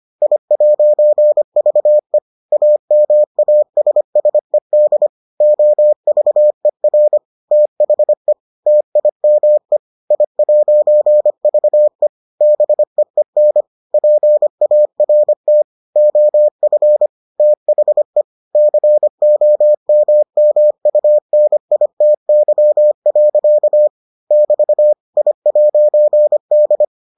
Morse code, heard as a keyed tone, about 25 wpm, 600 Hz. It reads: I'VE AMASSED OVER THE TIME I'VE BEEN PART OF THE COMMUNITY. <BT> I'D